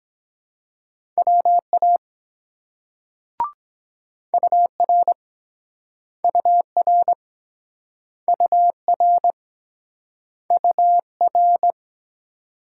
Morse code, heard as a keyed tone, about 23 wpm, 700 Hz. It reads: WA UR UR UR UR